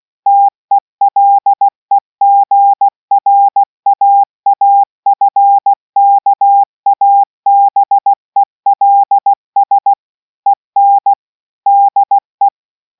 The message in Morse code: TELEGRAAFKABELS EN DE